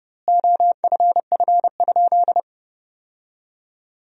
Morse code, heard as a keyed tone, about 30 wpm, 700 Hz. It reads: OFF?
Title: 2x Word-Spacing - Sentences from Top 500 Words - 30wpm